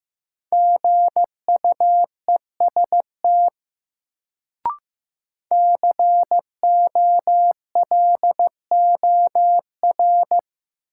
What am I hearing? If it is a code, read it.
Morse code, heard as a keyed tone, about 15 wpm, 700 Hz. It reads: GUEST COLOR
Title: Sentences from Top 1600 Words - New Word Drill Alt - 15wpm